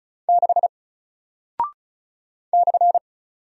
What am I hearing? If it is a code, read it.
Morse code, heard as a keyed tone, about 36 wpm, 700 Hz. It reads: 6 /
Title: Single Character (Letter:Number:Punctuation:Prosign) 35wpm